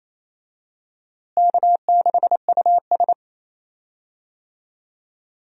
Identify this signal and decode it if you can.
Morse code, heard as a keyed tone, about 28 wpm, 700 Hz. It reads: K6US
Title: USA Call Signs - 4 Characters - Repeated 2x - 28wpm